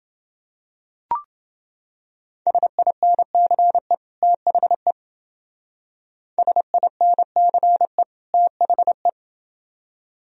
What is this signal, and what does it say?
Morse code, heard as a keyed tone, about 29 wpm, 700 Hz. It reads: SINCE THE SINCE THE